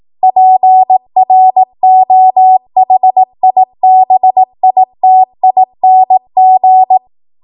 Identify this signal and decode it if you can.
Morse code, heard as a keyed tone, about 18 wpm, 750 Hz. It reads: PROHIBITING
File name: Morse / 18WPM